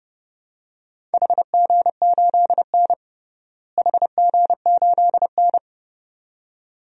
Morse code, heard as a keyed tone, about 30 wpm, 700 Hz. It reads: HG8N HG8N
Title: DXCC-100 Call Signs - 4 Characters - Repeated 2x - 30wpm